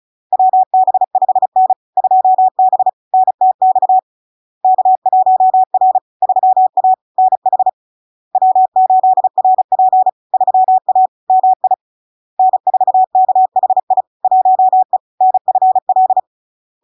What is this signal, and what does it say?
Morse code, heard as a keyed tone, about 36 wpm, 750 Hz. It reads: WB5N 2B NTX K1R 3A NH W8RP 3A MI N4KHI 1E NFL
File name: LARK-FD-2022-Log-in-CW-35wpm-750-Hz